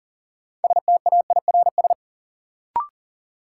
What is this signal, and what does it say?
Morse code, heard as a keyed tone, about 40 wpm, 700 Hz. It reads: STAIRS
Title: Sets of 5 Words Encore - New Word Drill - 40wpm